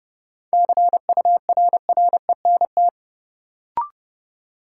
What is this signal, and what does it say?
Morse code, heard as a keyed tone, about 30 wpm, 700 Hz. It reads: CURRENT
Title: Sentences from Top 1600 Words - New Word Drill Alt - 30wpm